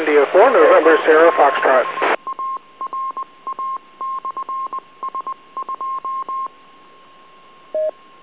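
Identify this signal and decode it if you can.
Morse code, heard as a keyed tone, about 20 wpm, spaced as for 18 wpm, 1060 Hz, with speech and radio noise. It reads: W7ARA/S2
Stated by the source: Autopatch tones used in a drill on making 9-1-1 calls through a repeater.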